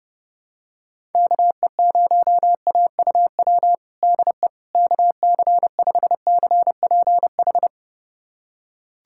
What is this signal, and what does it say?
Morse code, heard as a keyed tone, about 30 wpm, 700 Hz. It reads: KE0AUW DE KC5CPH
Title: USA Callsign DE Callsign - 6 Characters - 30wpm